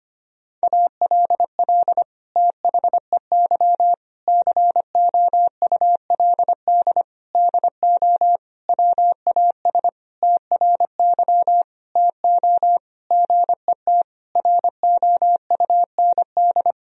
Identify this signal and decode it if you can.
Morse code, heard as a keyed tone, about 25 wpm, 700 Hz. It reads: ALL THEY COULD DO WAS TRY TO GET ROUND
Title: Sentences from Top 1000 Words - Review Part 1 - 25wpm